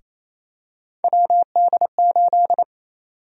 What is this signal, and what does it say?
Morse code, heard as a keyed tone, about 28 wpm, 700 Hz. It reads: WD8
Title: USA Call Sign Prefixes - Repeated 2x - 28wpm